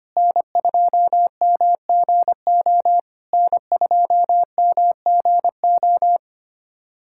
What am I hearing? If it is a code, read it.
Morse code, heard as a keyed tone, about 25 wpm, 700 Hz. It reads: N2MGO N2MGO
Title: SOTA QSO Practice 25wpm